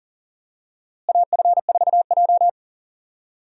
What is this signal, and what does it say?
Morse code, heard as a keyed tone, about 40 wpm, 700 Hz. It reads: AF4J
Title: USA Call Signs - 4 Characters - Repeated 2x - 40wpm